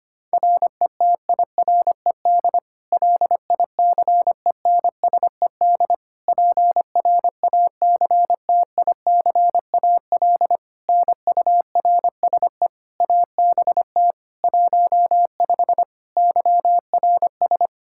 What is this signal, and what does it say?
Morse code, heard as a keyed tone, about 25 wpm, 700 Hz. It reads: RETIRED LICENSED PRACTICAL NURSE ABT 15 YRS
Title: K1USN SST QSO Practice 25wpm